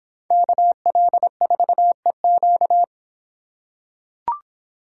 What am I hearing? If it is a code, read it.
Morse code, heard as a keyed tone, about 26 wpm, 700 Hz. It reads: KL4EQ